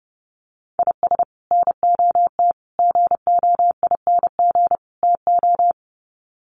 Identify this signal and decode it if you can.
Morse code, heard as a keyed tone, about 30 wpm, 700 Hz. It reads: IS NOT GOING TO
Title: Sets of 4 Words 30wpm